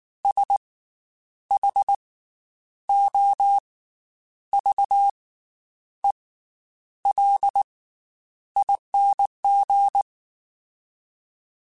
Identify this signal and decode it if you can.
Morse code, heard as a keyed tone, about 19 wpm, spaced as for 9 wpm, 800 Hz. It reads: SHOVELING
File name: Special INGs / Special INGs-mp3s / Slow ing @18